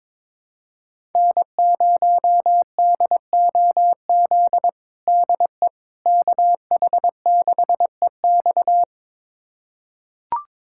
Morse code, heard as a keyed tone, about 22 wpm, 700 Hz. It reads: N0DOZ DE KH6EX
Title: USA Callsign DE Callsign - 5 Characters - 22wpm